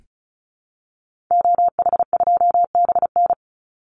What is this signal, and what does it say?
Morse code, heard as a keyed tone, about 34 wpm, 700 Hz. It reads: OH2BN
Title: Call Signs Repeated 2x - 35wpm